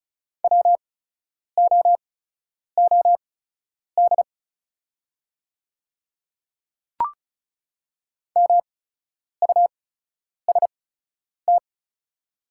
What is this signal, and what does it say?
Morse code, heard as a keyed tone, about 34 wpm, spaced as for 10 wpm, 700 Hz. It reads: WOOD MUST
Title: 8x Character Spacing - Top 300 Words - 35wpm